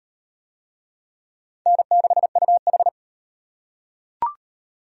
Morse code, heard as a keyed tone, about 38 wpm, 700 Hz. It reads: N6UH